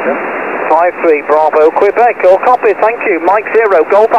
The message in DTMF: none